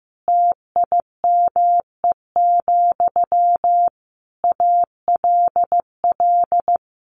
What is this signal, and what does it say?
Morse code, heard as a keyed tone, about 15 wpm, 700 Hz. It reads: TIME, ALL
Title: A Sound Of Thunder 15wpm